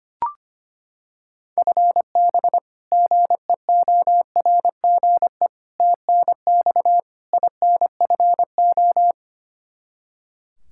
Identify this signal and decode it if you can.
Morse code, heard as a keyed tone, about 25 wpm, 700 Hz. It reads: FB GEORGE TNX INFO
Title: K1USN SST QSO Practice 25wpm